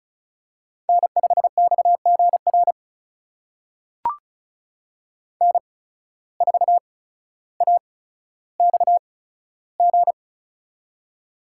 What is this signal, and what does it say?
Morse code, heard as a keyed tone, about 34 wpm, spaced as for 15 wpm, 700 Hz. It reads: N5XGR N4AXG
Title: USA Call Signs - 5 Characters - 8x Character Spacing - 35wpm